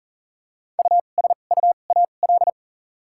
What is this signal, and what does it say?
Morse code, heard as a keyed tone, about 37 wpm, spaced as for 28 wpm, 700 Hz. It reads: USUAL
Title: 2x Character Spacing - Top 500 Words - 40wpm